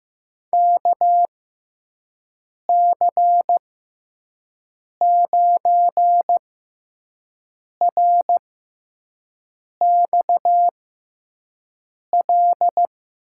Morse code, heard as a keyed tone, about 15 wpm, spaced as for 5 wpm, 700 Hz. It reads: KC9RXL
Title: USA Call Signs - 6 Characters - 6x Character Spacing - 15wpm